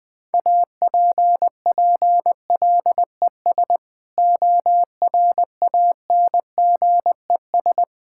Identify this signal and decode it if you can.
Morse code, heard as a keyed tone, about 20 wpm, 700 Hz. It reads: APPLES ORANGES
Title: Sentences with 6-Word Phrases - 20wpm